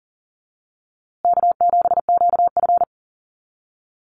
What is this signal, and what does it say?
Morse code, heard as a keyed tone, about 40 wpm, 700 Hz. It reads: K7QF
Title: USA Call Signs Repeated 2x 40wpm